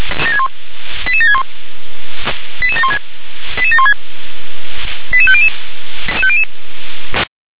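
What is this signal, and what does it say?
Five-tone selective call (ZVEI), heard as quick runs of five tones, tones of about 80 ms. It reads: 88761 88716 88716 88500
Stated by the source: ZVEI five-tone selective call received on an SDR.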